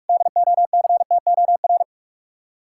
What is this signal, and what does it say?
Morse code, heard as a keyed tone, about 58 wpm, 700 Hz. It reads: DOCTOR